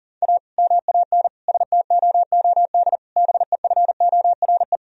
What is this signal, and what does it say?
Morse code, heard as a keyed tone, about 40 wpm, 700 Hz. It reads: A MAN STOOD BEFORE
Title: Sentences from Top 1000 Words - Review Part 2 - 40wpm